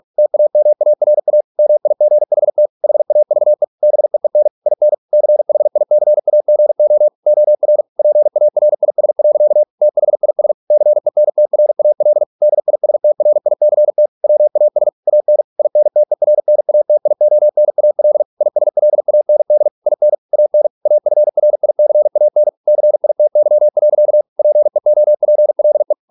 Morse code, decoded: TAMARA MIGHT HAVE BEEN IN CHICAGO OR PARIS. THIS CENTRAL DISTRICT WAS AN INTERNATIONAL ISLAND IN AN AFRICAN CITY. PEOPLE

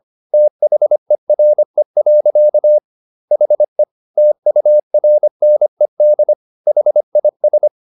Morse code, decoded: THERE. HE TURNED HIS

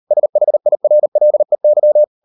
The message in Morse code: SHIRLEY

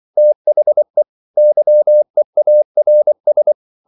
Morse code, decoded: THE YEARS

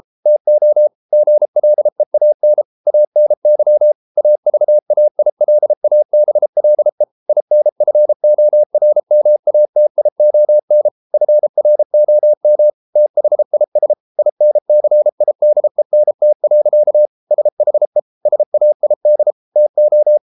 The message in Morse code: TO GLEAN ANY AVAILABLE INFORMATION FROM THIS INCIDENT. SHE SAID TO

